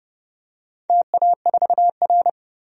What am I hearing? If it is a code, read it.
Morse code, heard as a keyed tone, about 31 wpm, 700 Hz. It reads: TA4R
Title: DXCC-100 Call Signs - 4 Characters - 30wpm